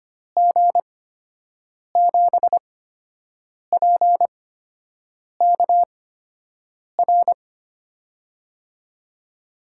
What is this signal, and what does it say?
Morse code, heard as a keyed tone, about 25 wpm, spaced as for 7 wpm, 700 Hz. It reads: G7PKR
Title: USA Call Signs - 8x Character Spacing - 25wpm